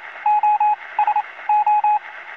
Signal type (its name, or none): Morse code